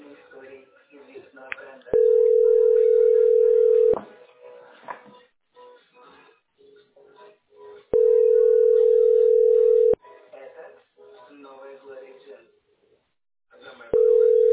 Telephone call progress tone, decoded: ringback tone

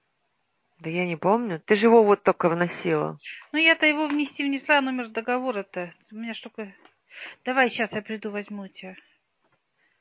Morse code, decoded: none

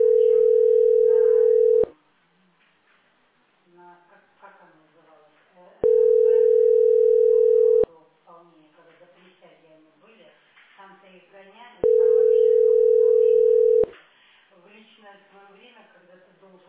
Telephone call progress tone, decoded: ringback tone

